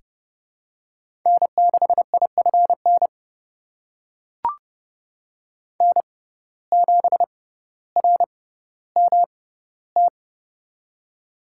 Morse code, heard as a keyed tone, about 30 wpm, spaced as for 15 wpm, 700 Hz. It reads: N6IFN N7RMT
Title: USA Call Signs - 5 Characters - 6x Character Spacing - 30wpm